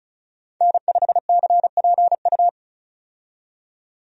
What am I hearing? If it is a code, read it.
Morse code, heard as a keyed tone, about 36 wpm, 700 Hz. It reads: N5CPU
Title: USA Call Signs - 5 Characters - 35wpm